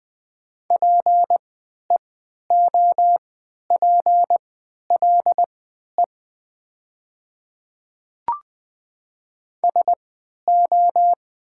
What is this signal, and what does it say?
Morse code, heard as a keyed tone, about 20 wpm, spaced as for 11 wpm, 700 Hz. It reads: PEOPLE SO
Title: 3x Character Spacing - Top 200 Words - 20wpm